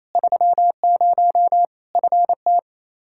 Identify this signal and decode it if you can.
Morse code, heard as a keyed tone, about 28 wpm, 700 Hz. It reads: 30 FT
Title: Practice QSO - Exchange 2 - Info - 28wpm